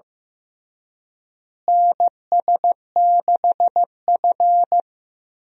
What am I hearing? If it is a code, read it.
Morse code, heard as a keyed tone, about 15 wpm, 700 Hz. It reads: NS6F
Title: USA Call Signs - 4 Characters - Repeated 3x - 15wpm